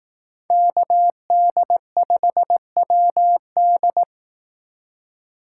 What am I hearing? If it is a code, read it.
Morse code, heard as a keyed tone, about 18 wpm, 700 Hz. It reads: KD5WD
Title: Speed Racing - USA 2x2 Call Signs 15wpm